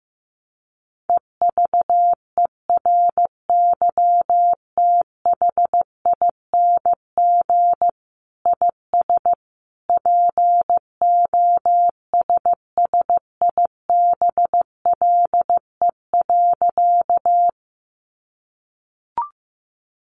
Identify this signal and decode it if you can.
Morse code, heard as a keyed tone, about 15 wpm, 700 Hz. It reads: EVERYTHING IS POSSIBLE.
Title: Sentences from Top 700 Words 15wpm